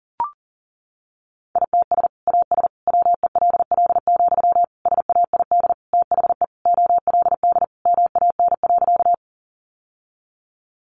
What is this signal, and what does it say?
Morse code, heard as a keyed tone, about 37 wpm, 700 Hz. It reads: ITS AS WELL, SAID THE OLD MAN.
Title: Sentences from Top 600 Words 40wpm